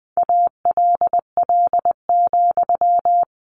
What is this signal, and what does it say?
Morse code, heard as a keyed tone, about 20 wpm, 700 Hz. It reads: ALL,
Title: A Sound Of Thunder 20wpm